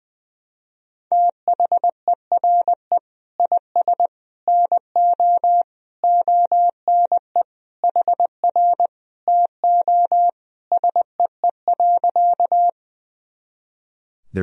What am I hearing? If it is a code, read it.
Morse code, heard as a keyed tone, about 20 wpm, 700 Hz. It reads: THERE IS NO ONE HR TO SEE.
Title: Sentences from Top 200 Words and Abbreviations 20wpm